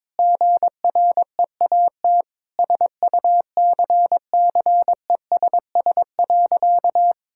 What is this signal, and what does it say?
Morse code, heard as a keyed tone, about 22 wpm, 700 Hz. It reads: GREAT SUCCESS.
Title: Sentences from Top 1000 Words - Review Part 6 - 22wpm